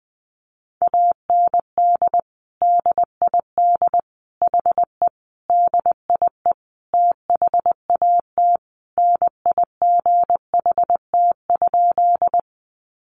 Morse code, heard as a keyed tone, about 20 wpm, 700 Hz. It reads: AND DID HE DIE THAT NIGHT?
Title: Sentences from Top 1300 Words 20wpm